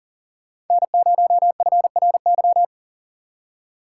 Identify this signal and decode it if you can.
Morse code, heard as a keyed tone, about 52 wpm, 700 Hz. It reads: N0FRY